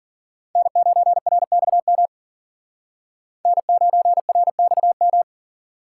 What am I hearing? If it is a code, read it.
Morse code, heard as a keyed tone, about 52 wpm, 700 Hz. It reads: N9RXM N9RXM